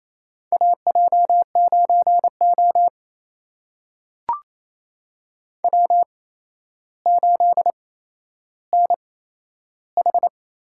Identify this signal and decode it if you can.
Morse code, heard as a keyed tone, about 28 wpm, spaced as for 12 wpm, 700 Hz. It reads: AJ9O W8NH
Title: USA Call Signs - 4 Characters - 8x Character Spacing - 28wpm